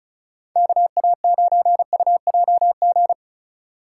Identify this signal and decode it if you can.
Morse code, heard as a keyed tone, about 34 wpm, 700 Hz. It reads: KA9UJG
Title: USA Call Signs - 6 Characters - 4x Character Spacing - 35wpm